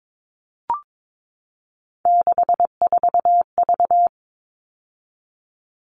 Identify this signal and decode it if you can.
Morse code, heard as a keyed tone, about 22 wpm, 700 Hz. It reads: 64V
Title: All Cut Numbers - 3-Digit 22wpm